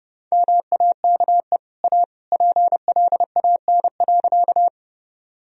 Morse code, heard as a keyed tone, about 30 wpm, 700 Hz. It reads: MAKE A PLAN.
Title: Sentences from Top 1000 Words - Review Part 2 - 30wpm